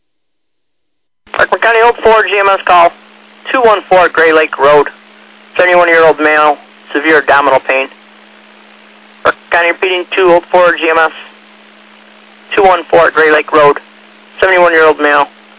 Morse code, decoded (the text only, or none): none